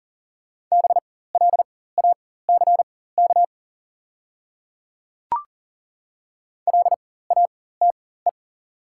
Morse code, heard as a keyed tone, about 40 wpm, spaced as for 19 wpm, 690 Hz. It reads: BLACK LATE
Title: 4x Character Spacing - Top 500 Words - 40wpm